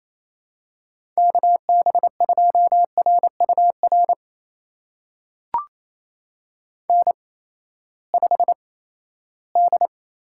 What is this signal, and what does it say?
Morse code, heard as a keyed tone, about 28 wpm, spaced as for 16 wpm, 700 Hz. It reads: KB2RUR N5D
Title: USA Call Signs - 8x Character Spacing - 28wpm